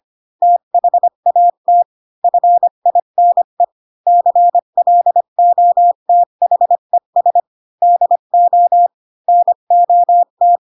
Morse code, heard as a keyed tone, about 25 wpm, 700 Hz. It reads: THAT FINE CLOTHES DO NOT
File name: Pinocchio / Pinocchio ch8 20 25 ik1hge